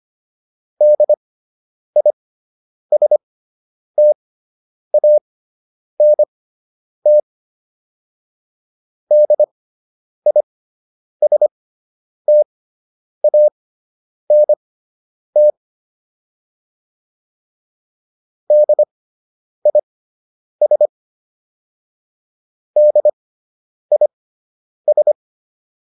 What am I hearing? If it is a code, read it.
Morse code, heard as a keyed tone, about 25 wpm, spaced as for 9 wpm, 600 Hz. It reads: DISTANT DISTANT DIS DIS